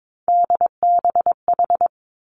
Morse code, heard as a keyed tone, about 22 wpm, 700 Hz. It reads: DBH
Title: Three Challenging Letter-Number Combinations 22wpm